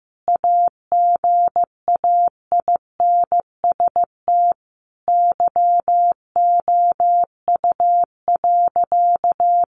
Morse code, heard as a keyed tone, about 15 wpm, 700 Hz. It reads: AGAINST YOU.
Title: Sentences from Top 1400 Words 15wpm